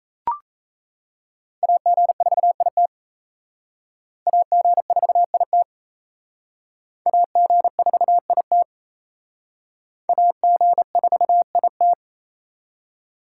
Morse code, heard as a keyed tone, about 37 wpm, 700 Hz. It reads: AG4IT AG4IT AG4IT AG4IT